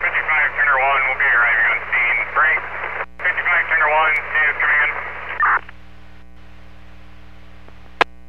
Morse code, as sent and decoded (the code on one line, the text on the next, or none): none